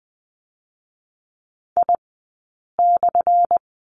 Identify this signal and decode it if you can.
Morse code, heard as a keyed tone, about 20 wpm, spaced as for 8 wpm, 700 Hz. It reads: I/
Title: 217 Lesson 35d - Number 0 - Two Characters 20wpm